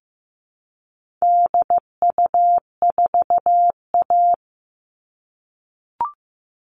Morse code, heard as a keyed tone, about 15 wpm, 700 Hz. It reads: DU4A
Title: All Cut Numbers - 4-Digit 15wpm